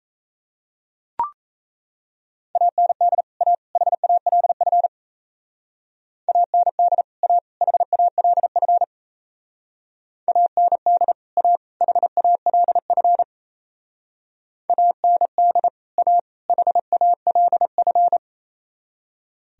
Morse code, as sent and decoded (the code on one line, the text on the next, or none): .- -. -.. / .- / .... .- .-.. ..-. / .- -. -.. / .- / .... .- .-.. ..-. / .- -. -.. / .- / .... .- .-.. ..-. / .- -. -.. / .- / .... .- .-.. ..-.
AND A HALF AND A HALF AND A HALF AND A HALF